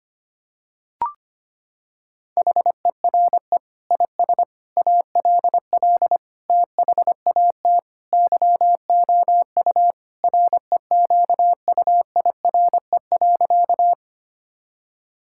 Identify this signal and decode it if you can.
Morse code, heard as a keyed tone, about 25 wpm, 700 Hz. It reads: HERE IS ALL THAT YOU REQUIRE.
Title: Sentences from Top 1000 Words - Review Part 6 - 25wpm